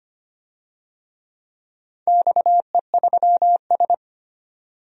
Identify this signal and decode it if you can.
Morse code, heard as a keyed tone, about 25 wpm, 700 Hz. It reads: XE3S
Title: DXCC-100 Call Signs - 4 Characters - 25wpm